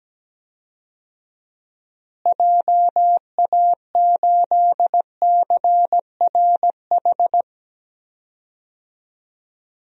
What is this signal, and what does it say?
Morse code, heard as a keyed tone, about 17 wpm, 700 Hz. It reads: JA8CRH
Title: DXCC-100 Call Signs - 6 Characters - Repeated 3x - 17wpm